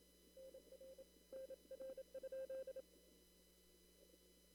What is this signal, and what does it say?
Morse code, heard as a keyed tone, about 27 wpm, 530 Hz. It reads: NR NR?
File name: CW / kt5j wpx cw-op-k5pi